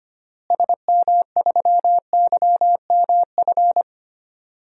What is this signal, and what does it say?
Morse code, heard as a keyed tone, about 25 wpm, 700 Hz. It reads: SM3YMF
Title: DXCC-100 Call Signs - 6 Characters - Repeated 2x - 25wpm